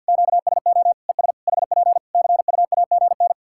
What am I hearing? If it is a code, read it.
Morse code, heard as a keyed tone, about 50 wpm, 700 Hz. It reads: QSO ES HP CUAGN
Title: K1USN SST QSO Practice 50wpm